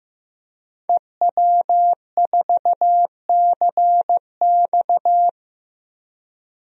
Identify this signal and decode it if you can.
Morse code, heard as a keyed tone, about 16 wpm, 700 Hz. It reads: EW4CX
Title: DXCC-100 Call Signs - 5 Characters - Repeated 2x - 15wpm